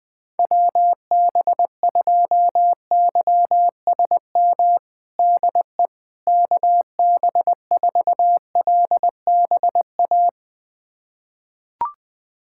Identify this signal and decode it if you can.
Morse code, heard as a keyed tone, about 20 wpm, 700 Hz. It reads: WB2YSM DE KB4LBA
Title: USA Callsign DE Callsign 20wpm